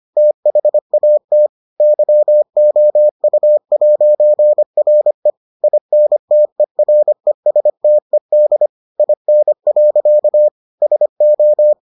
Morse code, heard as a keyed tone, about 25 wpm, 600 Hz. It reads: THAT YOU'RE INTERESTED IN. SO